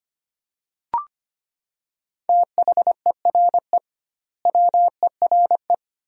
Morse code, heard as a keyed tone, about 25 wpm, 700 Hz. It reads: THERE WERE
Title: 2x Word-Spacing - Sentences from Top 100 Words - 25wpm